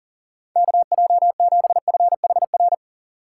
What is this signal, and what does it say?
Morse code, heard as a keyed tone, about 40 wpm, 700 Hz. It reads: KJ7FHR
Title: USA Call Signs - 6 Characters - 6x Character Spacing - 40wpm